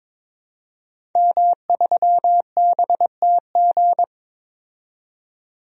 Morse code, heard as a keyed tone, about 22 wpm, 700 Hz. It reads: M3BTG
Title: DXCC-100 Call Signs - 5 Characters - 22wpm